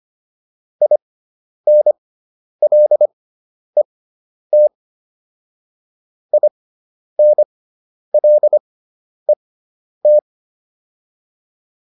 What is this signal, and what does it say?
Morse code, heard as a keyed tone, about 25 wpm, spaced as for 10 wpm, 600 Hz. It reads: INLET INLET